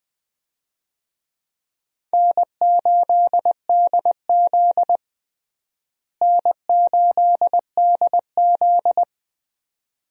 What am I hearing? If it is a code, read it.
Morse code, heard as a keyed tone, about 20 wpm, 700 Hz. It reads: N8DZ N8DZ
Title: USA Call Signs - 4 Characters - Repeated 2x - 20wpm